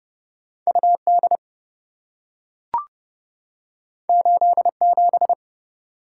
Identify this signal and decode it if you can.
Morse code, heard as a keyed tone, about 27 wpm, 700 Hz. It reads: UD 87